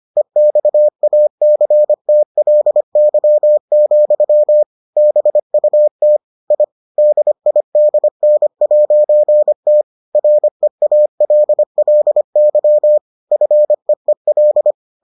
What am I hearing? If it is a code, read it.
Morse code, heard as a keyed tone, about 25 wpm, 600 Hz. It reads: EXACTLY, BUT I DIDN'T REALLY FEEL